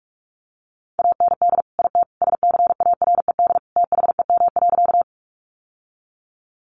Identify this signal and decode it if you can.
Morse code, heard as a keyed tone, about 44 wpm, 700 Hz. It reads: AND IT SCARED THEM.
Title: Sentences from Top 700 Words 45wpm